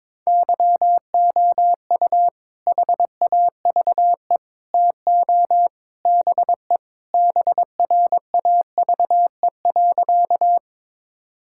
Morse code, heard as a keyed tone, about 21 wpm, 700 Hz. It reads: YOU HAVE TO BE BRAVE.
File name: practice / 22wpm / Sentences with 3-Word Phrases - 22wpm